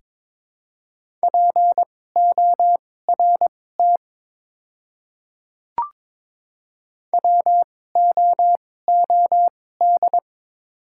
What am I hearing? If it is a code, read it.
Morse code, heard as a keyed tone, about 23 wpm, spaced as for 16 wpm, 700 Hz. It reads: PORT WOOD
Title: 2x Character Spacing - Top 300 Words - 22wpm